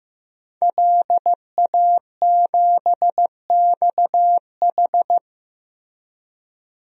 Morse code, heard as a keyed tone, about 15 wpm, 700 Hz. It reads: LA7XH
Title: DXCC-100 Call Signs - 5 Characters - Repeated 3x - 15wpm